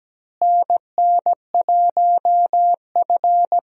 Morse code, heard as a keyed tone, about 17 wpm, 700 Hz. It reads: NN1F
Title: USA Call Signs - 4 Characters - Repeated 2x - 17wpm